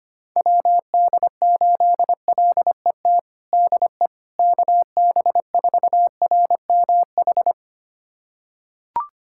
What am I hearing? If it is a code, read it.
Morse code, heard as a keyed tone, about 25 wpm, 700 Hz. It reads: WD8LET DE KB4RMH
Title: USA Callsign DE Callsign 25wpm